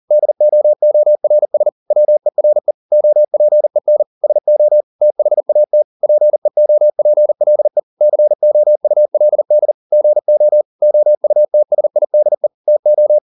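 Morse code, heard as a keyed tone, about 40 wpm, 600 Hz. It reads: DOORS WERE OPEN SO THAT PEOPLE COULD GO OUTSIDE TO
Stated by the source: Morse code at 40 wpm, practice text, CW med Gnister.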